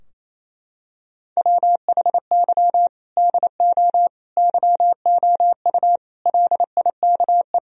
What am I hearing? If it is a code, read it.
Morse code, heard as a keyed tone, about 28 wpm, 700 Hz. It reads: WHY DO YOU LIKE
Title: Sentences from Top 1000 Words - Review Part 4 - 28wpm